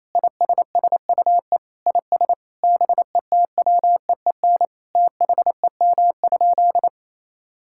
Morse code, heard as a keyed tone, about 27 wpm, 700 Hz. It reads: ISSUE IS BETWEEN THEM?
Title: Sentences from Top 1000 Words - Review Part 6 - 28wpm